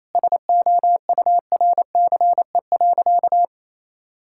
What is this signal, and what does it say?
Morse code, heard as a keyed tone, about 28 wpm, 700 Hz. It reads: SOURCE.
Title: Sentences from Top 1000 Words - Review Part 5 - 28wpm